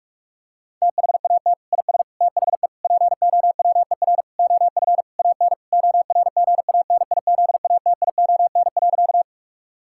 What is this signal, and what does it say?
Morse code, heard as a keyed tone, about 42 wpm, 700 Hz. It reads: THAT IS THE POWER OF AN ORGANIZATION.
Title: Sentences from Top 1000 Words - Review Part 5 - 45wpm